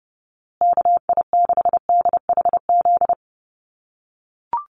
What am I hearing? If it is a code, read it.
Morse code, heard as a keyed tone, about 30 wpm, 700 Hz. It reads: KI6DHZ
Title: USA Call Signs Repeated 2x 30wpm